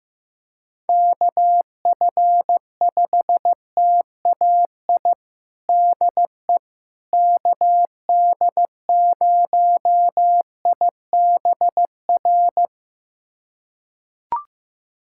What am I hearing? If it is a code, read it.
Morse code, heard as a keyed tone, about 16 wpm, 700 Hz. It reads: KF5TAI DE KD0IBR
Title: USA Callsign DE Callsign 15wpm